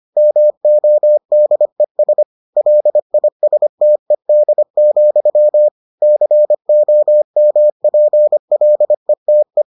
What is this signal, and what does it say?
Morse code, heard as a keyed tone, about 25 wpm, 600 Hz. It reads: MODES LISTED, COMPLETE